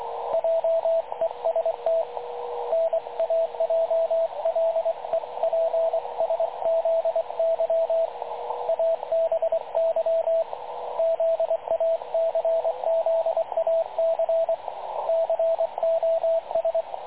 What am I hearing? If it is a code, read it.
Morse code, heard as a keyed tone, about 22 wpm, 660 Hz, with radio noise. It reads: JEST NAJLEPSZY ABY ZACZAC COS
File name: dlazaawansowanych / nagraniazpasma / websdr recording 3567.0kHz